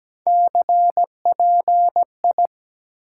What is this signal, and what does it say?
Morse code, heard as a keyed tone, about 17 wpm, 700 Hz. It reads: CPI
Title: Top 90 QSO Elements 17wpm